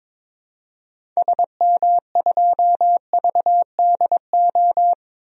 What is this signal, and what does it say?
Morse code, heard as a keyed tone, about 21 wpm, 700 Hz. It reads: SM2VDO
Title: DXCC-100 Call Signs - 6 Characters - 22wpm